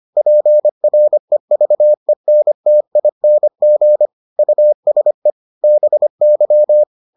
Morse code, heard as a keyed tone, about 25 wpm, 600 Hz. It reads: PREVENTING USE BY